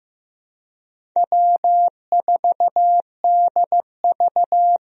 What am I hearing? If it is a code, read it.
Morse code, heard as a keyed tone, about 15 wpm, 700 Hz. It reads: W4DV
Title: USA Call Signs - 4 Characters - Repeated 2x - 15wpm